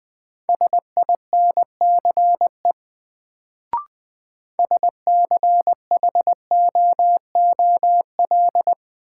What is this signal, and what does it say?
Morse code, heard as a keyed tone, about 20 wpm, 700 Hz. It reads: SINCE SCHOOL
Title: Sentences from Top 300 Words - New Word Drill - 20wpm